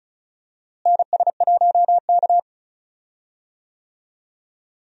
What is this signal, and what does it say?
Morse code, heard as a keyed tone, about 34 wpm, 700 Hz. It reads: NS1K
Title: USA Call Signs - 4 Characters - Repeated 3x - 35wpm